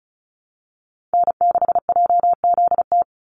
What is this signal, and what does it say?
Morse code, heard as a keyed tone, about 34 wpm, 700 Hz. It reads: N6JZT
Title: NA SOTA Activator Callsigns 35wpm